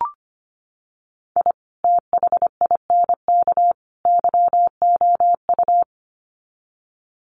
Morse code, heard as a keyed tone, about 25 wpm, 700 Hz. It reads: I THINK YOU
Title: Sets of 3 Words 25wpm